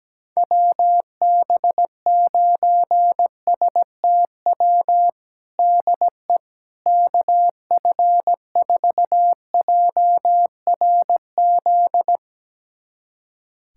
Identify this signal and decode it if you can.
Morse code, heard as a keyed tone, about 17 wpm, 700 Hz. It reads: WB9STW DE KF4JRZ
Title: USA Callsign DE Callsign 17wpm